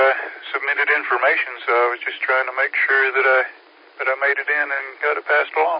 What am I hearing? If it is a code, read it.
Emergency Alert System (SAME) tones (header bursts), heard as harsh buzzing bursts, with speech. It is heard from 0.5 to 5.0 s.